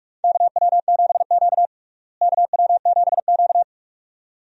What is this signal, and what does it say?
Morse code, heard as a keyed tone, about 41 wpm, 700 Hz. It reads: KW7Q KW7Q
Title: USA Call Signs - 4 Characters - Repeated 2x - 45wpm